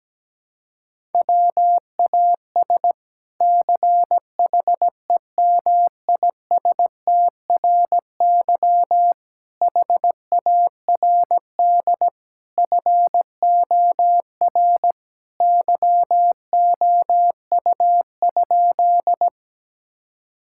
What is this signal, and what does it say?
Morse code, heard as a keyed tone, about 17 wpm, 700 Hz. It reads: WAS CHEMISTRY HARD FOR YOU?
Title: Sentences from Top 1000 Words - Review Part 6 - 17wpm